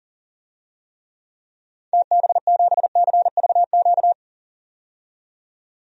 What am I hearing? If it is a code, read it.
Morse code, heard as a keyed tone, about 40 wpm, 700 Hz. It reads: TB7CVQ
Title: DXCC-100 Call Signs - 6 Characters - 40wpm